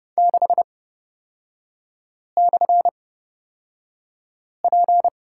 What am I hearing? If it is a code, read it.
Morse code, heard as a keyed tone, about 30 wpm, spaced as for 5 wpm, 700 Hz. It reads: 6/P
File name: Lesson 37e - Letter x - Three Characters 30wpm